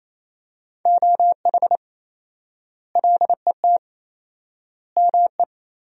Morse code, heard as a keyed tone, about 28 wpm, 700 Hz. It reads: OH LET ME
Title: 4x Word-Spacing - Sentences from Top 500 Words - 28wpm